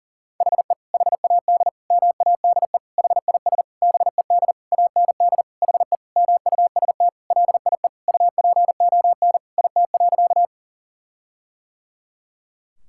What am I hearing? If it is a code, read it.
Morse code, heard as a keyed tone, about 40 wpm, 700 Hz. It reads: HE HAD MADE HIS BED AND HE MUST LIE UPON IT.
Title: Sentences from Top 1600 Words 40wpm